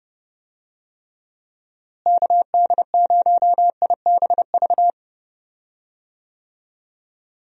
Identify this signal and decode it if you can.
Morse code, heard as a keyed tone, about 30 wpm, 700 Hz. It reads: KD0IBV
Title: USA Call Signs - 6 Characters - Repeated 2x - 30wpm